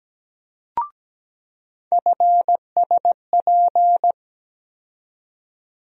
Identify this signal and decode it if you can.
Morse code, heard as a keyed tone, about 17 wpm, 700 Hz. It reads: FSP
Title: USA Call Sign Postfixes - 3 Letters 17wpm